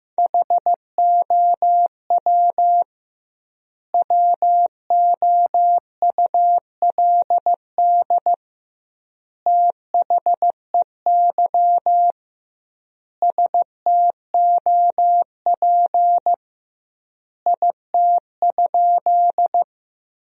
Morse code, heard as a keyed tone, about 15 wpm, 700 Hz. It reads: HOW WOULD THEY STOP IT?
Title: 2x Word-Spacing - Sentences from Top 300 Words - 15wpm